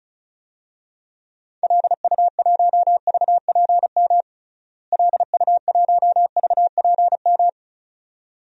Morse code, heard as a keyed tone, about 36 wpm, 700 Hz. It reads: LU1VPM LU1VPM
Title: DXCC-100 Call Signs - 6 Characters - Repeated 2x - 35wpm